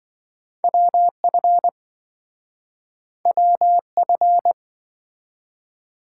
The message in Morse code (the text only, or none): WF WF